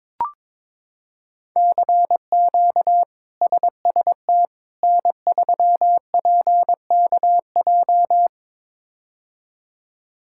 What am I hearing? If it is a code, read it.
Morse code, heard as a keyed tone, about 22 wpm, 700 Hz. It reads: CQ SST N3PKJ
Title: K1USN SST QSO Practice - 22wpm